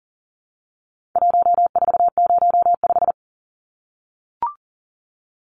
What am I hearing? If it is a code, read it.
Morse code, heard as a keyed tone, about 40 wpm, 700 Hz. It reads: 1405
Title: Numbers - 4-Digit 40wpm